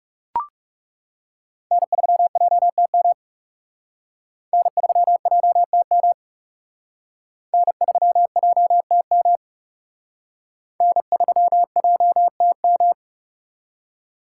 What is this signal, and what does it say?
Morse code, heard as a keyed tone, about 37 wpm, 700 Hz. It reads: N3JTM N3JTM N3JTM N3JTM
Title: Speed Racing - USA 1x3 Call Signs 30wpm